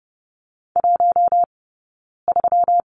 Morse code, heard as a keyed tone, about 30 wpm, 700 Hz. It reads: 1 3
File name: Lesson 20d - Number 3 - Two Characters 30wpm